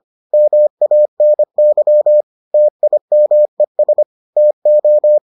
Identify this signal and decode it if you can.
Morse code, heard as a keyed tone, about 25 wpm, 600 Hz. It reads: MANY TIMES TO